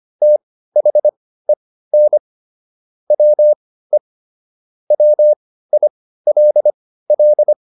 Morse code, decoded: THEN WE WILL

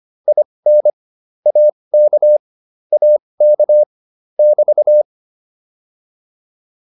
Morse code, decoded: IN AK AK <BT>

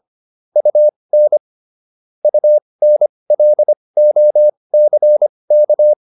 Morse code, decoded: UN UNLOCK

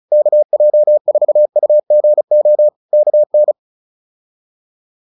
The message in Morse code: KJ4UGO KN

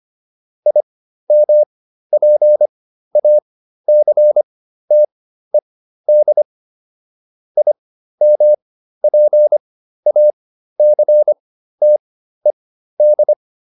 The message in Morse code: IMPACTED IMPACTED

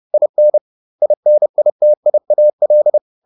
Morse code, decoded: IN INITIAL

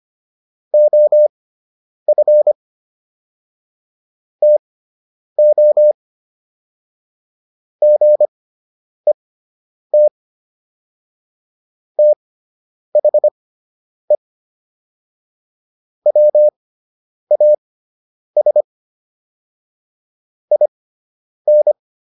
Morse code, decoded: OF TO GET THE WAS IN

